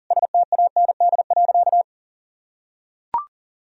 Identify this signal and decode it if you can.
Morse code, heard as a keyed tone, about 40 wpm, 700 Hz. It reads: STAND.
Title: Sentences from Top 1000 Words - Review Part 2 - 40wpm